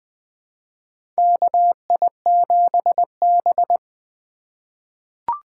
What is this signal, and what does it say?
Morse code, heard as a keyed tone, about 20 wpm, 700 Hz. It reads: KI7B
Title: USA Call Signs - 4 Characters - 3x Character Spacing - 20wpm